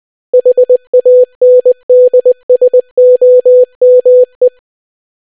Morse code, heard as a keyed tone, about 20 wpm, 500 Hz. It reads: HANDSOME